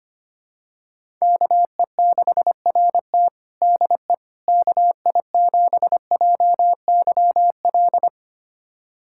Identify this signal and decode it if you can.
Morse code, heard as a keyed tone, about 25 wpm, 700 Hz. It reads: KE6RT DE KI7JYL
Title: USA Callsign DE Callsign 25wpm